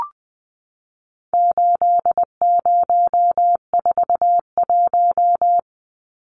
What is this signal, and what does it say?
Morse code, heard as a keyed tone, about 20 wpm, 700 Hz. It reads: 8041
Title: Numbers - 4-Digit 20wpm